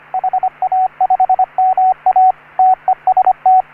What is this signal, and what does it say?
Morse code, heard as a keyed tone, about 27 wpm, 750 Hz, with radio noise. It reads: HA5MA TEST
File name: CW / HA3OU RHOBCW REC